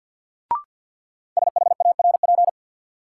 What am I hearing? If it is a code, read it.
Morse code, heard as a keyed tone, about 53 wpm, 700 Hz. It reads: SHARP
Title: Sentences from Top 1600 Words - New Word Drill Alt - 50wpm